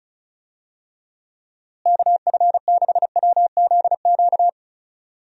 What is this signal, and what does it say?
Morse code, heard as a keyed tone, about 34 wpm, 700 Hz. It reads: KF6WZQ
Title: USA Call Signs - 6 Characters - Repeated 3x - 35wpm